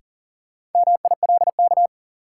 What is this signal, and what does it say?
Morse code, heard as a keyed tone, about 40 wpm, 700 Hz. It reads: MILK
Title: Sentences from Top 1500 Words - New Word Drill Alt - 40wpm